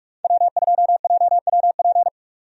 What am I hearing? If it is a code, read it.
Morse code, heard as a keyed tone, about 43 wpm, 700 Hz. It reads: W2JWP
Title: USA Call Signs - 5 Characters - 45wpm